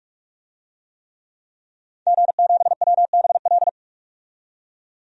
Morse code, heard as a keyed tone, about 45 wpm, 700 Hz. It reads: G7WBL
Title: DXCC-100 Call Signs - 5 Characters - 45wpm